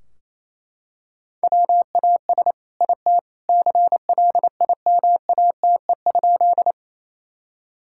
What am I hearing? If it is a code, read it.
Morse code, heard as a keyed tone, about 28 wpm, 700 Hz. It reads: WAS IT CLIMATE?
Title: Sentences from Top 1600 Words 28wpm